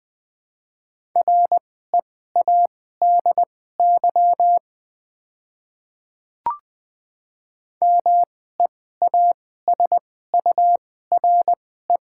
Morse code, heard as a keyed tone, about 20 wpm, spaced as for 15 wpm, 700 Hz. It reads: READY MEASURE
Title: 2x Character Spacing - Top 500 Words - 20wpm